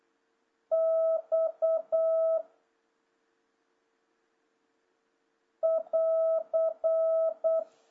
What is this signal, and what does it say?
Morse code, heard as a keyed tone, about 8 wpm, 650 Hz. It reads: X <AR>
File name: jh8cbh / 2026cw